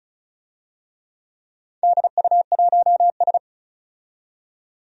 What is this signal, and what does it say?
Morse code, heard as a keyed tone, about 36 wpm, 700 Hz. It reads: DU1S
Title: DXCC-100 Call Signs - 4 Characters - 35wpm